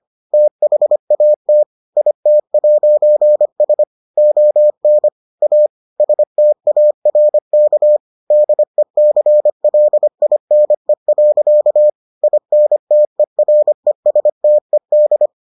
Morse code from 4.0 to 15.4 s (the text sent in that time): ON A STARK DECLINE. INTERESTED